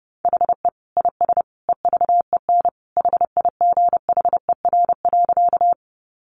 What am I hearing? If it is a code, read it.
Morse code, heard as a keyed tone, about 30 wpm, 700 Hz. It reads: HE IS EVEN HIGHER.
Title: Additional Easy Phrases 30wpm